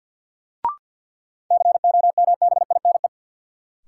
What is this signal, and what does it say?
Morse code, heard as a keyed tone, about 52 wpm, 700 Hz. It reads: COMBINE
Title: Sentences from Top 1600 Words - New Word Drill Alt - 50wpm